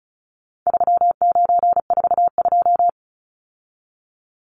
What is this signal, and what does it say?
Morse code, heard as a keyed tone, about 34 wpm, 700 Hz. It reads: 3942
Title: Numbers - 4-Digit 35wpm